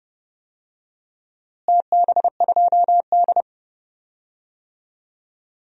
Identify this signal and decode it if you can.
Morse code, heard as a keyed tone, about 30 wpm, 700 Hz. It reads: TB2D
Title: DXCC-100 Call Signs - 4 Characters - Repeated 3x - 30wpm